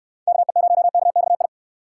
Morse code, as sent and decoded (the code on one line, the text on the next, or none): -.- . ----- -.- -..- ..
KE0KXI